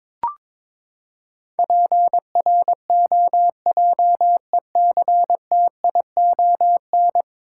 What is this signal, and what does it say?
Morse code, heard as a keyed tone, about 22 wpm, 700 Hz. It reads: PROJECTION
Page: morse-code-ninja-practice-22wpm